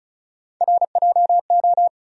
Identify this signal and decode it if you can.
Morse code, heard as a keyed tone, about 34 wpm, 700 Hz. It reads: RJO
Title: USA Call Sign Postfixes - 3 Letters 35wpm